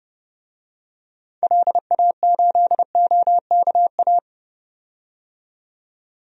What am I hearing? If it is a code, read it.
Morse code, heard as a keyed tone, about 30 wpm, 700 Hz. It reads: LA8OKA
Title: DXCC-100 Call Signs - 6 Characters - 30wpm